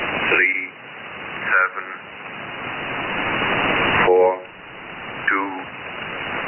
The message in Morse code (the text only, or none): none